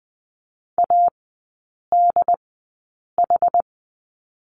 Morse code, heard as a keyed tone, about 20 wpm, spaced as for 8 wpm, 690 Hz. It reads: ADH
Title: 081 Lesson 14e - Number 2 - Three Characters 20wpm